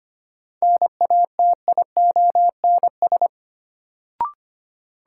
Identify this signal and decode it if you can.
Morse code, heard as a keyed tone, about 25 wpm, 700 Hz. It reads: NATIONS
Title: Sets of 3 Words - New Word Drill - 25wpm